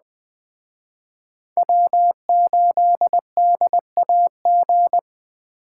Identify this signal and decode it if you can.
Morse code, heard as a keyed tone, about 20 wpm, 700 Hz. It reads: W8DAG
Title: Speed Racing - USA 1x3 Call Signs 17wpm